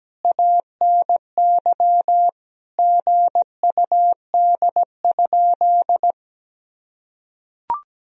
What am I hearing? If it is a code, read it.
Morse code, heard as a keyed tone, about 17 wpm, 700 Hz. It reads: ANY GUD?
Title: Sentences from Top 200 Words and Abbreviations 17wpm